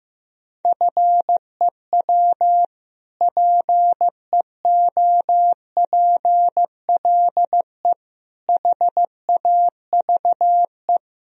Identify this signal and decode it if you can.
Morse code, heard as a keyed tone, about 15 wpm, 700 Hz. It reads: FEW PEOPLE HAVE
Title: Sentences from Top 1000 Words - Review Part 1 - 15wpm